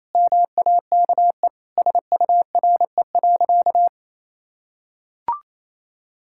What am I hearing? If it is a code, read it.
Morse code, heard as a keyed tone, about 28 wpm, 700 Hz. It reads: MAKE SURE.
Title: Sentences from Top 1000 Words - Review Part 5 - 28wpm